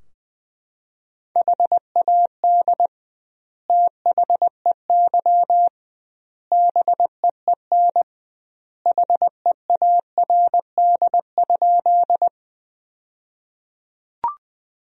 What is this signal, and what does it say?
Morse code, heard as a keyed tone, about 20 wpm, 700 Hz. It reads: HAD THEY BEEN HEARD?
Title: 2x Word-Spacing - Sentences from Top 400 Words - 20wpm